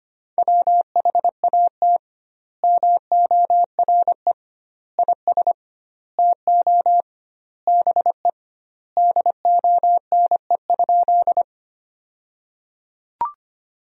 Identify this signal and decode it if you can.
Morse code, heard as a keyed tone, about 25 wpm, 700 Hz. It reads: WHAT MORE IS TO BE DONE?
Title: 2x Word-Spacing - Sentences from Top 500 Words - 25wpm